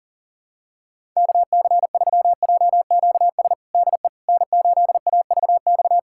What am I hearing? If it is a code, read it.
Morse code, heard as a keyed tone, about 40 wpm, 700 Hz. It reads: KC3JQS DE N8AVX
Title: USA Callsign DE Callsign 40wpm